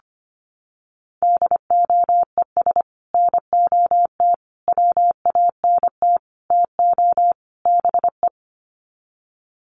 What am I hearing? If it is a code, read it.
Morse code, heard as a keyed tone, about 25 wpm, 700 Hz. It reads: DOES NOT WANT TO BE
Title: Sets of 5 Words Encore 25wpm